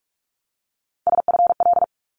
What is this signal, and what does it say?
Morse code, heard as a keyed tone, about 46 wpm, 700 Hz. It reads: SFL
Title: ARRL Sections 45wpm